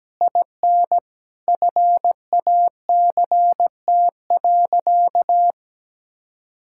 Morse code, heard as a keyed tone, about 17 wpm, 700 Hz. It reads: IN FACT.
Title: Sentences from Top 1000 Words - Review Part 6 - 17wpm